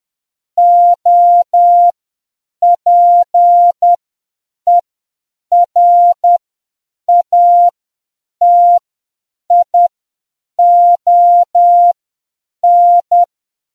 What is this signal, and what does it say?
Morse code, heard as a keyed tone, about 10 wpm, spaced as for 7 wpm, 700 Hz. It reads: OPERATION